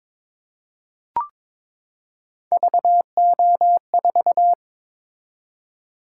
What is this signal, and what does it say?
Morse code, heard as a keyed tone, about 22 wpm, 700 Hz. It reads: VO4
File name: practice / 22wpm / Three Letter-Number Combinations - 22wpm